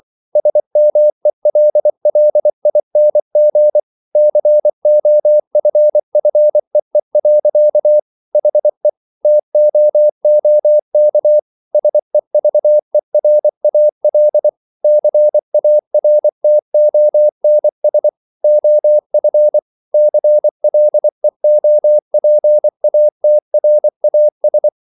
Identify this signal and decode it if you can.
Morse code, heard as a keyed tone, about 24 wpm, 600 Hz. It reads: SMELLING COFFEE. HE TOOK SEVERAL CARTONS OF CLEOPATRAS